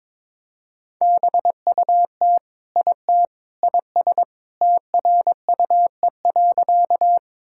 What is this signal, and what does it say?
Morse code, heard as a keyed tone, about 22 wpm, 700 Hz. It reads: BUT IT IS TRUE.
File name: practice / 22wpm / Sentences with 3-Word Phrases - 22wpm